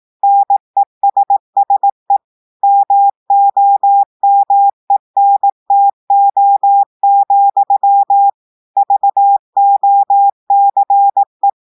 Morse code, heard as a keyed tone, about 18 wpm, 800 Hz. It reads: NESSE MOMENTO, VOCE
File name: audios / portugais / portugais texte en clair 2 vitesse 18 mots minutes